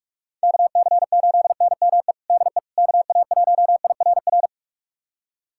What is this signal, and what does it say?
Morse code, heard as a keyed tone, about 42 wpm, 700 Hz. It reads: KC8NME DE KA1IRR
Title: USA Callsign DE Callsign - 6 Characters - 45wpm